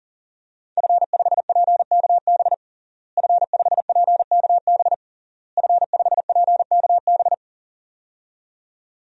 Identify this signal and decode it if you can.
Morse code, heard as a keyed tone, about 40 wpm, 700 Hz. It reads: F5PKB F5PKB F5PKB
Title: DXCC-100 Call Signs - 5 Characters - Repeated 3x - 40wpm